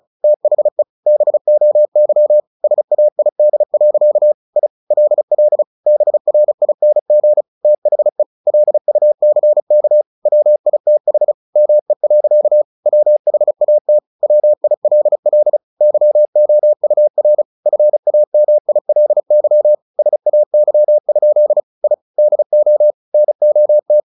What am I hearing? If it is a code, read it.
Morse code, heard as a keyed tone, about 34 wpm, 600 Hz. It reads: THE BOY SAID. I LL BRING THE LUCK WITH ME. WHAT WILL YOUR FAMILY SAY? I DO NOT
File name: the old man and the sea 2 35wpm